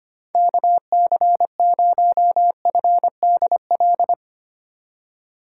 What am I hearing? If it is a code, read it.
Morse code, heard as a keyed tone, about 25 wpm, 700 Hz. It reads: KC0FDL
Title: USA Call Signs - 6 Characters - 4x Character Spacing - 25wpm